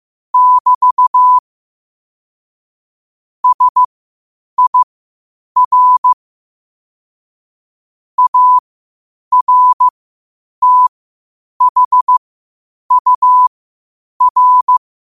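Morse code, heard as a keyed tone, about 15 wpm, spaced as for 8 wpm, 1000 Hz. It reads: <BT> SIR ARTHUR
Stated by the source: Morse code quotes at 8 words per minute from QOTD at 8 WPM.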